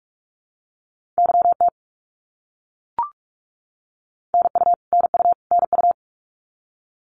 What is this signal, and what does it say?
Morse code, heard as a keyed tone, about 44 wpm, 700 Hz. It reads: YT NU NU NU
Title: Canadian Provinces and Territories - Repeated 3x - 45wpm